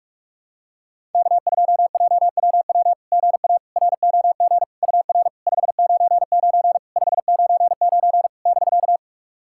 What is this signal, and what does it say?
Morse code, heard as a keyed tone, about 46 wpm, 690 Hz. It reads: K2JWW GA ROG UR 599 599 <BK>
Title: SOTA QSO Practice 45wpm